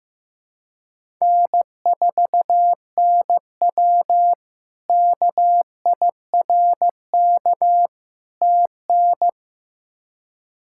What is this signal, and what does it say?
Morse code, heard as a keyed tone, about 15 wpm, 700 Hz. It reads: N4NW KIRK TN
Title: K1USN SST QSO Practice - 15wpm